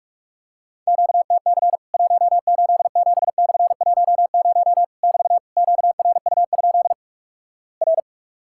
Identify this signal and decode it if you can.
Morse code, heard as a keyed tone, about 47 wpm, 700 Hz. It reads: QTC 187/10 <BT> QRU? R
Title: Worked All Europe DX Contest - QTC Exchange - 45wpm